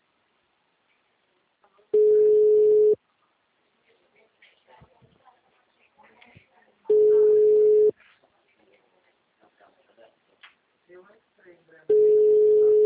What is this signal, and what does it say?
Ringback tone (Europe, 425 Hz), with speech.
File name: out- -2007- - - .22095